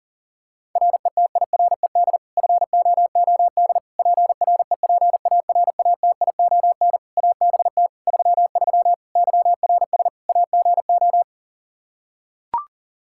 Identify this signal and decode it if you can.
Morse code, heard as a keyed tone, about 40 wpm, 700 Hz. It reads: RETIRED FOOD PREPARATION ABT 33 YRS AGO
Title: K1USN SST QSO Practice 40wpm